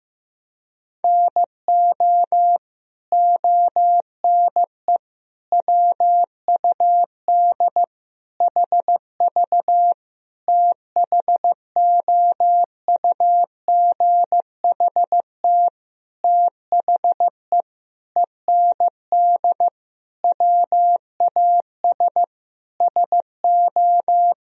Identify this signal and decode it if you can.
Morse code, heard as a keyed tone, about 15 wpm, 700 Hz. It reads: NO ONE WUD HV THOUGHT THE END WAS SO
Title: Sentences from Top 300 Words and Abbreviations 15wpm